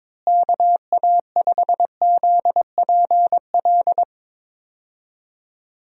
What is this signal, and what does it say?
Morse code, heard as a keyed tone, about 22 wpm, 700 Hz. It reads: KA5ZPL
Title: USA Call Signs - 6 Characters - Repeated 2x - 22wpm